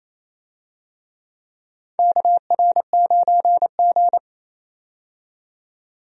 Morse code, heard as a keyed tone, about 28 wpm, 690 Hz. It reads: KR9G
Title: USA Call Signs - 4 Characters - 28wpm